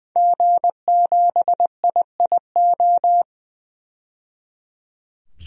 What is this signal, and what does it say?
Morse code, heard as a keyed tone, about 21 wpm, 700 Hz. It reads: G7IIO
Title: DXCC-100 Call Signs - 5 Characters - 20wpm